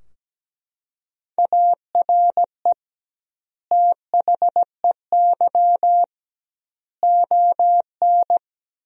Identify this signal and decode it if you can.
Morse code, heard as a keyed tone, about 17 wpm, 700 Hz. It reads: ARE THEY ON
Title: 2x Word-Spacing - Sentences from Top 500 Words - 17wpm